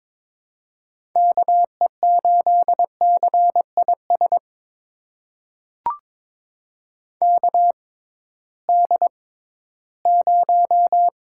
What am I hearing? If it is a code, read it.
Morse code, heard as a keyed tone, about 22 wpm, spaced as for 14 wpm, 700 Hz. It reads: KE8CIS KD0
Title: USA Call Signs - 6 Characters - 6x Character Spacing - 22wpm